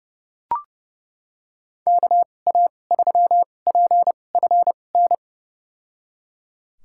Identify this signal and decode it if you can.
Morse code, heard as a keyed tone, about 30 wpm, spaced as for 22 wpm, 700 Hz. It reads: KA3PFN
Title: USA Call Signs - 6 Characters - 2x Character Spacing - 30wpm